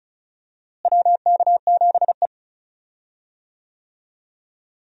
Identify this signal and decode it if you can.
Morse code, heard as a keyed tone, about 34 wpm, 700 Hz. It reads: WK7E
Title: USA Call Signs - 4 Characters - Repeated 3x - 35wpm